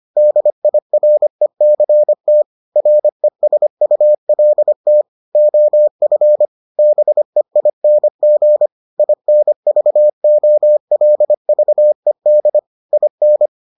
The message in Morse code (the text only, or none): DIRECT RESULT OF BEING INVOLVED IN